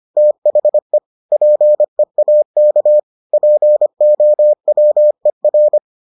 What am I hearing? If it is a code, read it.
Morse code, heard as a keyed tone, about 25 wpm, 600 Hz. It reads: THE PEAK POWER